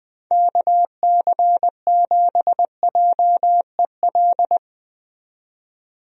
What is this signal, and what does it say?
Morse code, heard as a keyed tone, about 20 wpm, 700 Hz. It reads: KC7JEL
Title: USA Call Signs - 6 Characters - 20wpm